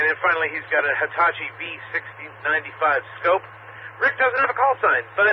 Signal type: Emergency Alert System (SAME) tones